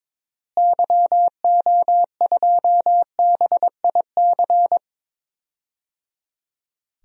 Morse code, heard as a keyed tone, about 21 wpm, 700 Hz. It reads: YO2BIC
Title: DXCC-100 Call Signs - 6 Characters - Repeated 3x - 22wpm